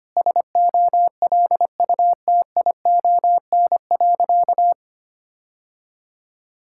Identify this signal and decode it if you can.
Morse code, heard as a keyed tone, about 25 wpm, 700 Hz. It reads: SOLUTION.
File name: practice / 25wpm / Sentences with 3-Word Phrases - 25wpm